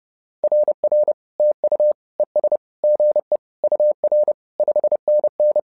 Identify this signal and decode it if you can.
Morse code, heard as a keyed tone, about 30 wpm, 600 Hz. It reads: RR TU ES GE UR 5NN
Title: SOTA QSO Practice 30wpm